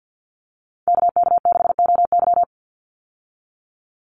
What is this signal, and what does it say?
Morse code, heard as a keyed tone, about 50 wpm, 700 Hz. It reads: KK6OY
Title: USA Call Signs 50wpm